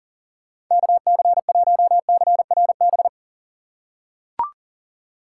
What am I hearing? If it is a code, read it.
Morse code, heard as a keyed tone, about 40 wpm, 700 Hz. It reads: KC1CRB
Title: USA Call Signs - 2x Character Spacing - 40wpm